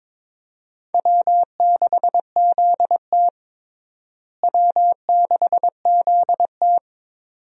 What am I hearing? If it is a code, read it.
Morse code, heard as a keyed tone, about 22 wpm, 700 Hz. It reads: W6ZT W6ZT
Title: USA Call Signs - 4 Characters - Repeated 2x - 22wpm